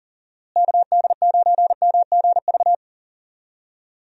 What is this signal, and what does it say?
Morse code, heard as a keyed tone, about 40 wpm, 700 Hz. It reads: KD9MGV
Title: USA Call Signs - 6 Characters - Repeated 2x - 40wpm